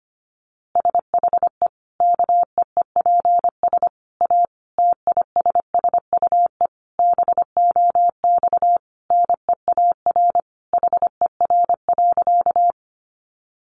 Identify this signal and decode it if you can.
Morse code, heard as a keyed tone, about 25 wpm, 700 Hz. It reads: SHE KEEPS A TISSUE BOX NEAR HER.
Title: Easy Phrases 25wpm